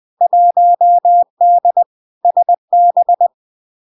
Morse code, heard as a keyed tone, about 20 wpm, 700 Hz. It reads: 1D SB
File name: LARK-FD-2022-Log-in-CW-20wpm-700-Hz